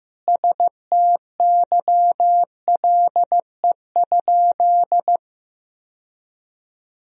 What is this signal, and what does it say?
Morse code, heard as a keyed tone, about 15 wpm, 700 Hz. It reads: STYLE?
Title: Sentences from Top 1000 Words - Review Part 4 - 15wpm